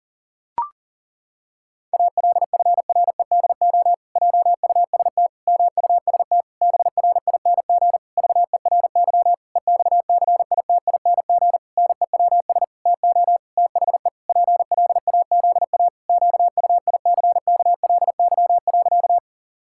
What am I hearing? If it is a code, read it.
Morse code, heard as a keyed tone, about 40 wpm, 700 Hz. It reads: ALFREDO JUST MUST BRING VERY EXCITING NEWS TO THE PLAZA QUICKLY.
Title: Pangram Challenge v2 - 40wpm